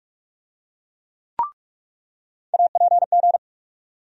Morse code, heard as a keyed tone, about 47 wpm, 700 Hz. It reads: APG
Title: USA Call Sign Postfixes - 3 Letters 45wpm